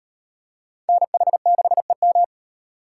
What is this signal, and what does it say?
Morse code, heard as a keyed tone, about 38 wpm, 700 Hz. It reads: NH6EM